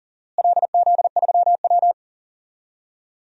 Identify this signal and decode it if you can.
Morse code, heard as a keyed tone, about 39 wpm, 700 Hz. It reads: LZ3W
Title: DXCC-100 Call Signs - 4 Characters - Repeated 3x - 40wpm